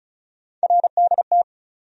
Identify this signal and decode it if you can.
Morse code, heard as a keyed tone, about 36 wpm, 700 Hz. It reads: RDT
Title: USA Call Sign Postfixes - 3 Letters 35wpm